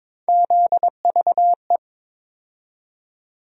Morse code, heard as a keyed tone, about 22 wpm, 700 Hz. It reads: ZVE
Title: USA Call Sign Postfixes - 3 Letters 22wpm